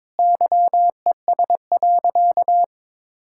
Morse code, heard as a keyed tone, about 22 wpm, 700 Hz. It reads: YES.
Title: Sentences from Top 1000 Words - Review Part 5 - 22wpm